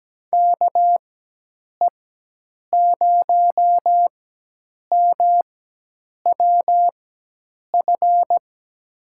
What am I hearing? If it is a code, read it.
Morse code, heard as a keyed tone, about 17 wpm, spaced as for 8 wpm, 700 Hz. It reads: KE0MWF
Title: USA Call Signs - 6 Characters - 4x Character Spacing - 17wpm